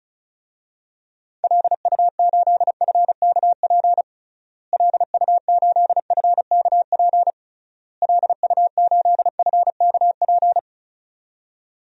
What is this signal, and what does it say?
Morse code, heard as a keyed tone, about 36 wpm, 700 Hz. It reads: LU8FKP LU8FKP LU8FKP
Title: DXCC-100 Call Signs - 6 Characters - Repeated 3x - 35wpm